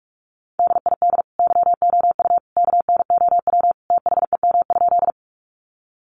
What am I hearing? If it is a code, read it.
Morse code, heard as a keyed tone, about 44 wpm, 700 Hz. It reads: DID YOU KNOW THEM?
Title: Sentences from Top 100 Words 45wpm